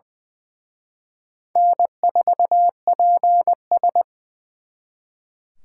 Morse code, heard as a keyed tone, about 23 wpm, 700 Hz. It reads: N4PS
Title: Speed Racing - USA 1x2 Call Signs 20wpm